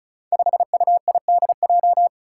Morse code, heard as a keyed tone, about 36 wpm, 700 Hz. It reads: 5UIDJ
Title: Five Challenging Letter-Number Combinations 35wpm